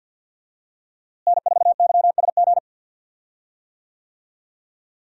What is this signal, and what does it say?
Morse code, heard as a keyed tone, about 50 wpm, 700 Hz. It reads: N4YSG
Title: USA Call Signs - 5 Characters - Repeated 3x - 50wpm